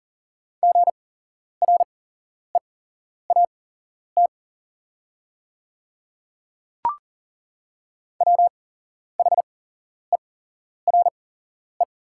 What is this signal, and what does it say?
Morse code, heard as a keyed tone, about 38 wpm, spaced as for 11 wpm, 690 Hz. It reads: GREAT WHERE
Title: 8x Character Spacing - Top 300 Words - 40wpm